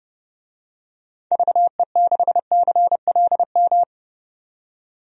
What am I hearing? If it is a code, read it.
Morse code, heard as a keyed tone, about 30 wpm, 700 Hz. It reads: VE6CLM
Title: DXCC-100 Call Signs - 6 Characters - 30wpm